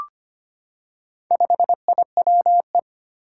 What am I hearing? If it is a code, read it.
Morse code, heard as a keyed tone, about 25 wpm, 700 Hz. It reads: 5IWE
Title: Four Challenging Letter-Number Combinations 25wpm